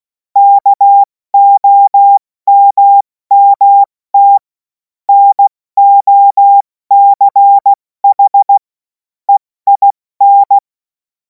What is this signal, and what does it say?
Morse code, heard as a keyed tone, about 16 wpm, 800 Hz. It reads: KOMMT NOCH EIN
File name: Morsekurs / Witze / Witz 17. März 2026 70 F80